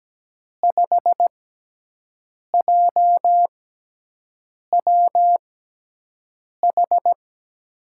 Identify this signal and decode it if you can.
Morse code, heard as a keyed tone, about 17 wpm, spaced as for 6 wpm, 700 Hz. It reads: 5JWH
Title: USA Call Signs - 5 Characters - 6x Character Spacing - 17wpm